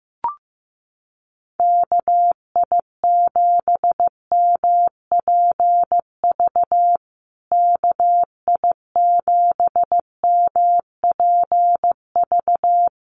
Morse code, heard as a keyed tone, about 15 wpm, 700 Hz. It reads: KI7MPV KI7MPV
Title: USA Call Signs Repeated 3x 15wpm